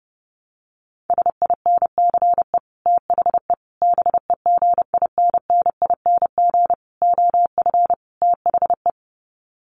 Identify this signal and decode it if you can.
Morse code, heard as a keyed tone, about 30 wpm, 700 Hz. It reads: SINCE THE BEGINNING OF THE
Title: Sets of 5 Words Encore2 30wpm